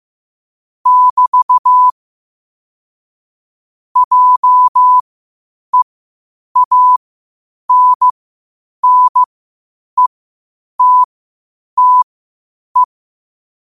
Morse code, heard as a keyed tone, about 15 wpm, spaced as for 8 wpm, 1000 Hz. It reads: <BT> JEANNETTE
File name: 8wpm